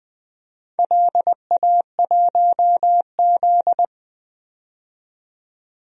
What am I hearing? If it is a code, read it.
Morse code, heard as a keyed tone, about 20 wpm, 700 Hz. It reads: LA1Z
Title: DXCC-100 Call Signs - 4 Characters - 20wpm